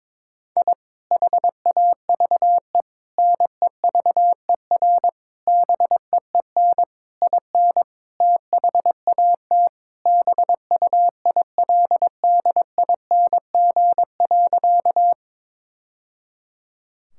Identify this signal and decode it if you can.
Morse code, heard as a keyed tone, about 22 wpm, 700 Hz. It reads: I HAVE NEVER BEEN IN THAT BUILDING.
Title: Sentences from Top 1000 Words - Review Part 5 - 22wpm